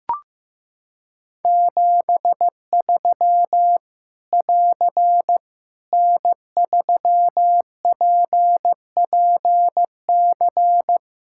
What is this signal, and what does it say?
Morse code, heard as a keyed tone, about 15 wpm, 700 Hz. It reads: 73 <AR> N3PPC
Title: Practice QSO - Exchange 4 - Ending - 15wpm